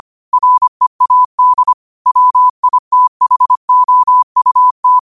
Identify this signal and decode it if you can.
Morse code, heard as a keyed tone, about 25 wpm, 1000 Hz. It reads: READ WITHOUT
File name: qotdfiles / 25wpm